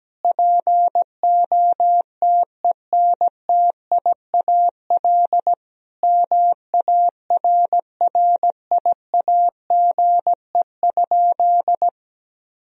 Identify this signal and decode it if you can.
Morse code, heard as a keyed tone, about 17 wpm, 700 Hz. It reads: POTENTIAL MARRIAGE?
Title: Sentences from Top 1000 Words - Review Part 4 - 17wpm